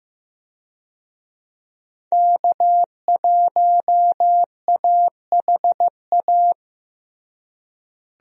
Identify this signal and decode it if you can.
Morse code, heard as a keyed tone, about 15 wpm, 700 Hz. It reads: K1AHA
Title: USA Call Signs - 5 Characters - Repeated 3x - 15wpm